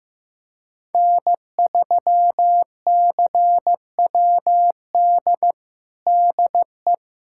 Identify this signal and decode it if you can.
Morse code, heard as a keyed tone, about 15 wpm, 700 Hz. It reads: N3CWD DE
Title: USA Callsign DE Callsign - 5 Characters - 15wpm